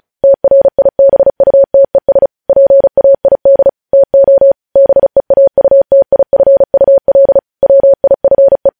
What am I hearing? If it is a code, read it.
Morse code, heard as a keyed tone, about 36 wpm, 550 Hz. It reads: TRIBUTES PAID TO BEAUTIFUL WIFE
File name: podcast / News headlines-1x-35x35